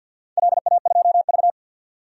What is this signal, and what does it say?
Morse code, heard as a keyed tone, about 51 wpm, 700 Hz. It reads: LA2V
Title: DXCC-100 Call Signs - 4 Characters - 50wpm